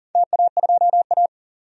Morse code, heard as a keyed tone, about 40 wpm, 700 Hz. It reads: TA2A
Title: DXCC-100 Call Signs - 4 Characters - 40wpm